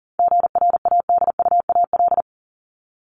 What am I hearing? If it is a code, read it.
Morse code, heard as a keyed tone, about 40 wpm, 700 Hz. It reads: GRADUAL
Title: LY Suffix - Top 100 Words with Base Words 40wpm